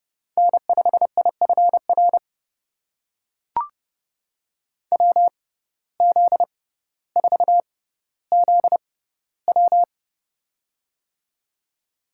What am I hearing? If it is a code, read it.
Morse code, heard as a keyed tone, about 30 wpm, spaced as for 15 wpm, 700 Hz. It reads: N5IFR WZ4ZW
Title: USA Call Signs - 5 Characters - 6x Character Spacing - 30wpm